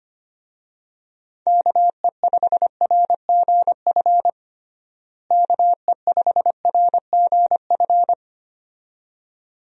Morse code, heard as a keyed tone, about 25 wpm, 700 Hz. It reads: KE5RGF KE5RGF
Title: USA Call Signs - 6 Characters - Repeated 2x - 25wpm